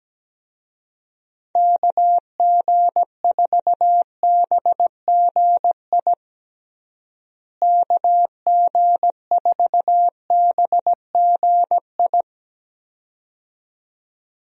Morse code, heard as a keyed tone, about 17 wpm, 700 Hz. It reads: KG4BGI KG4BGI
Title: USA Call Signs - 6 Characters - Repeated 2x - 17wpm